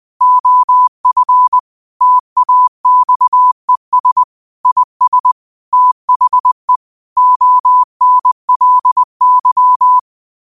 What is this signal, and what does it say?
Morse code, heard as a keyed tone, about 20 wpm, 1000 Hz. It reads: OF TAXES IS THE ONLY